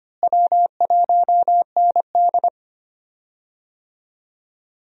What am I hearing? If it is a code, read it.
Morse code, heard as a keyed tone, about 25 wpm, 700 Hz. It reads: W1ND
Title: USA Call Signs - 4 Characters - Repeated 2x - 25wpm